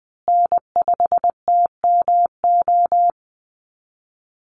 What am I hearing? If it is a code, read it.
Morse code, heard as a keyed tone, about 20 wpm, 700 Hz. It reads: N5TMO
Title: USA Call Signs Repeated 3x 20wpm